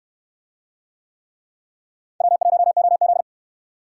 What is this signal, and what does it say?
Morse code, heard as a keyed tone, about 69 wpm, 700 Hz. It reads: W1OZ